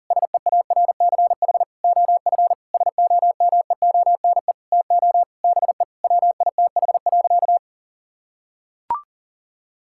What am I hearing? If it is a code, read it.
Morse code, heard as a keyed tone, about 40 wpm, 700 Hz. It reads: SEARCH OF SOMEONE TO BE WITH.
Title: Sentences from Top 1000 Words - Review Part 5 - 40wpm